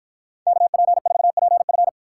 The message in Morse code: KP4PF

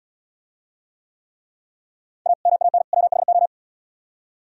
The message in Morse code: A MAN WHO

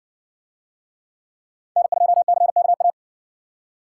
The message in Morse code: N2CKA

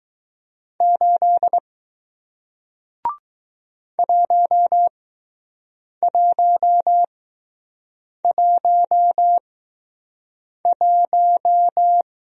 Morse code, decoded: 8 1111

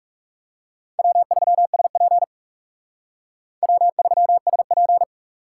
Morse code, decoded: W3SP W3SP

